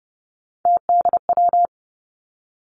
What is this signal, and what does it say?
Morse code, heard as a keyed tone, about 30 wpm, 700 Hz. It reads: TDW
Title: Three Challenging Letter-Number Combinations 30wpm